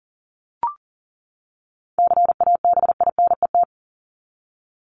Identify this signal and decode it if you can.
Morse code, heard as a keyed tone, about 38 wpm, 700 Hz. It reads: CABINET
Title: Top 1751-2000 Words 40wpm